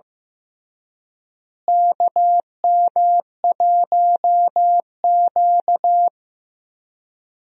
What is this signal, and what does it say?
Morse code, heard as a keyed tone, about 16 wpm, 700 Hz. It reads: KM1Q
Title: USA Call Signs - 4 Characters - Repeated 3x - 15wpm